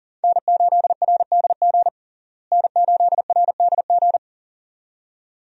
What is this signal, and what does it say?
Morse code, heard as a keyed tone, about 40 wpm, 700 Hz. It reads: N8RDG N8RDG
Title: USA Call Signs - 5 Characters - Repeated 2x - 40wpm